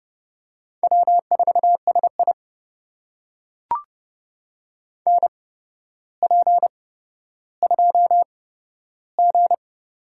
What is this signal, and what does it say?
Morse code, heard as a keyed tone, about 30 wpm, spaced as for 13 wpm, 700 Hz. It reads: W4SI NP2G
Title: USA Call Signs - 4 Characters - 8x Character Spacing - 30wpm